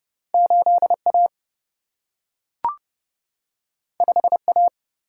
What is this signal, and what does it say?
Morse code, heard as a keyed tone, about 30 wpm, 700 Hz. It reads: 8 A 5 A
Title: Single-Letter Single-Number 30wpm